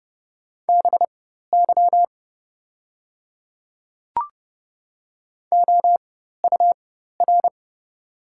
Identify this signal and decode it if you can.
Morse code, heard as a keyed tone, about 30 wpm, spaced as for 14 wpm, 700 Hz. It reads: BY OUR
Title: 4x Character Spacing - Top 200 Words - 30wpm